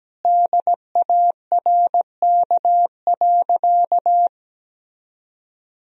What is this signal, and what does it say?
Morse code, heard as a keyed tone, about 17 wpm, 700 Hz. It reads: DARK.
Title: Sentences from Top 500 Words and Abbreviations 17wpm